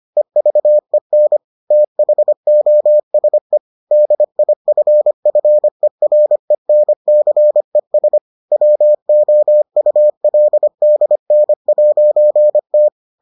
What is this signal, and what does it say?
Morse code, heard as a keyed tone, about 25 wpm, 600 Hz. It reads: EVEN THOSE DIFFERENCES WOULDN'T